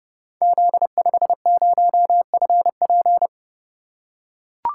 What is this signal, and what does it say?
Morse code, heard as a keyed tone, about 28 wpm, 700 Hz. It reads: Z50FP
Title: Code Groups - 5 Letter:Numbers - 30wpm